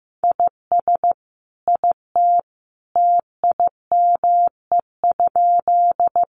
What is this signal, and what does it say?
Morse code, heard as a keyed tone, about 15 wpm, 700 Hz. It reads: IS IT TIME?
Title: Sentences from Top 100 Words 15wpm